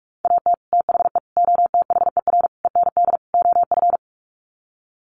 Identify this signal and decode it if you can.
Morse code, heard as a keyed tone, about 44 wpm, 700 Hz. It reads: AT THE OTHER END OF
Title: Sets of 5 Words 45wpm